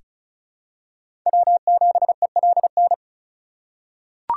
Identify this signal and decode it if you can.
Morse code, heard as a keyed tone, about 36 wpm, 700 Hz. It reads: W7ELN
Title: USA Call Signs - 5 Characters - 2x Character Spacing - 35wpm